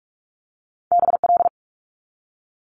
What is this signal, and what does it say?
Morse code, heard as a keyed tone, about 46 wpm, 700 Hz. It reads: BL
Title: Multi-letter Phonemes 45wpm